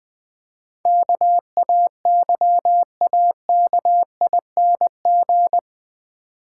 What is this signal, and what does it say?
Morse code, heard as a keyed tone, about 20 wpm, 700 Hz. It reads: KAYAKING
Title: Difficult Words - Speed-Racing with Hints - 20wpm